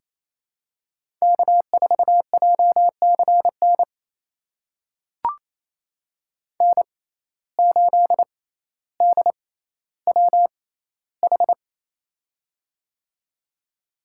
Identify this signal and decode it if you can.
Morse code, heard as a keyed tone, about 28 wpm, spaced as for 14 wpm, 700 Hz. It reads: K4JCN N8DWH
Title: USA Call Signs - 5 Characters - 6x Character Spacing - 28wpm